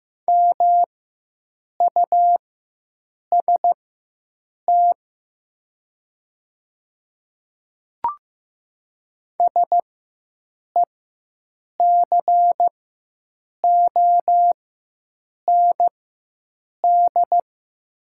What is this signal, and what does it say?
Morse code, heard as a keyed tone, about 15 wpm, spaced as for 7 wpm, 700 Hz. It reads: MUST SECOND
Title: 4x Character Spacing - Top 300 Words - 15wpm